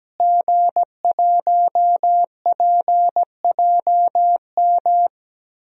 Morse code, heard as a keyed tone, about 17 wpm, 700 Hz. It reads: G1PJM
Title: DXCC-100 Call Signs - 5 Characters - Repeated 3x - 17wpm